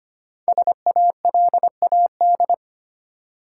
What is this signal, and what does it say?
Morse code, heard as a keyed tone, about 25 wpm, 700 Hz. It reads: SALAD
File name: 5 Letter Words - First 3 Letters Given - 25wpm